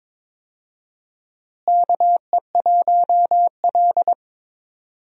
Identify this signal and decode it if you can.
Morse code, heard as a keyed tone, about 22 wpm, 700 Hz. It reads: KE1L
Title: USA Call Signs - 4 Characters - Repeated 2x - 22wpm